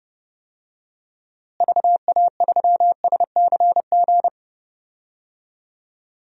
Morse code, heard as a keyed tone, about 30 wpm, 700 Hz. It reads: VA3SCG
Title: DXCC-100 Call Signs - 6 Characters - Repeated 2x - 30wpm